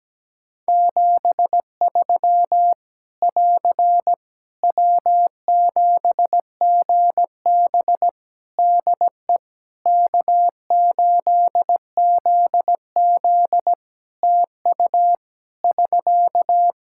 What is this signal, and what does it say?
Morse code, heard as a keyed tone, about 17 wpm, 700 Hz. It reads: 73 <AR> W7GB DE K8ZZ TU <SK>
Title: Practice QSO - Exchange 4 - Ending - 17wpm